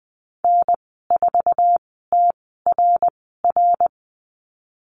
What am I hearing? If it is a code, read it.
Morse code, heard as a keyed tone, about 20 wpm, spaced as for 14 wpm, 700 Hz. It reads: N4TRR
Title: 076 Lesson 13g - Letter L - Call Signs 20wpm